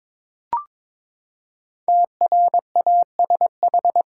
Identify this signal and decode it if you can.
Morse code, heard as a keyed tone, about 21 wpm, 700 Hz. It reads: TRASH
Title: Five Letter Words - 22wpm